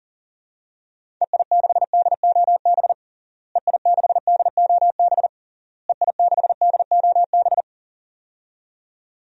Morse code, heard as a keyed tone, about 37 wpm, 700 Hz. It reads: EI6DOB EI6DOB EI6DOB
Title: DXCC-100 Call Signs - 6 Characters - Repeated 3x - 40wpm